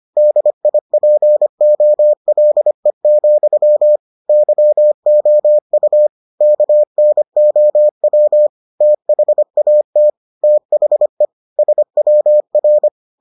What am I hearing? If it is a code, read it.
Morse code, heard as a keyed tone, about 25 wpm, 600 Hz. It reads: DIPOLE, YOU KNOW THAT THE SWR